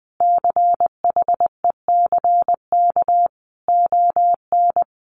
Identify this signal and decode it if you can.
Morse code, heard as a keyed tone, about 20 wpm, 700 Hz. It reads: CHECK ON
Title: Sentences from Top 1400 Words 20wpm